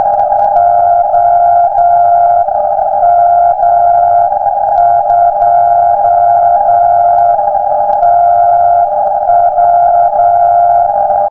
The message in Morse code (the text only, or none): OM2TW